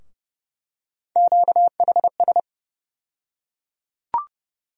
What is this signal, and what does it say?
Morse code, heard as a keyed tone, about 30 wpm, 700 Hz. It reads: QHS
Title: USA Call Sign Postfixes - 3 Letters 30wpm